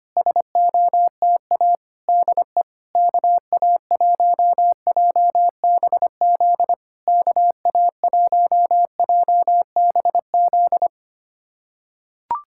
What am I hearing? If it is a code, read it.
Morse code, heard as a keyed tone, about 25 wpm, 700 Hz. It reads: SOTA DE KA1JBZ KA1JBZ
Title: SOTA QSO Practice 25wpm